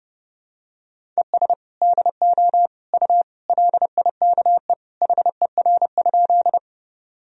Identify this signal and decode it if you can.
Morse code, heard as a keyed tone, about 30 wpm, 700 Hz. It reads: ES DO U LIKE HER?
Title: Sentences from Top 100 Words and Abbreviations 30wpm